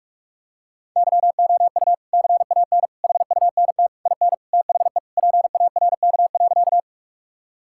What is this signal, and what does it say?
Morse code, heard as a keyed tone, about 46 wpm, 700 Hz. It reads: YOU CAN HUNT IN THE PARK.
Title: Sentences from Top 1600 Words 45wpm